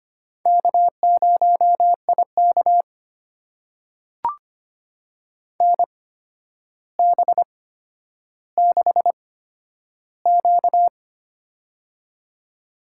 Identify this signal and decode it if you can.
Morse code, heard as a keyed tone, about 25 wpm, spaced as for 11 wpm, 700 Hz. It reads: K0IK NB6Q
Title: USA Call Signs - 4 Characters - 8x Character Spacing - 25wpm